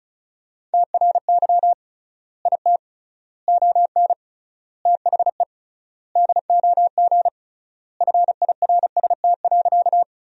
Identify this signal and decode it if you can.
Morse code, heard as a keyed tone, about 34 wpm, 700 Hz. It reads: TRY IT ON THE DOG FIRST.
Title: 3x Word-Spacing - Sentences from Top 400 Words - 35wpm